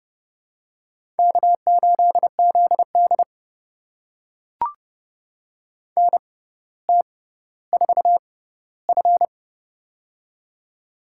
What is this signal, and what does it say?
Morse code, heard as a keyed tone, about 30 wpm, spaced as for 15 wpm, 700 Hz. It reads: K8ZD NT4F
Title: USA Call Signs - 4 Characters - 6x Character Spacing - 30wpm